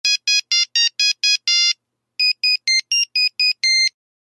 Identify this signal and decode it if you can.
There is no code.